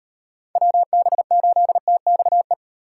Morse code, heard as a keyed tone, about 35 wpm, 700 Hz. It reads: WB8TXE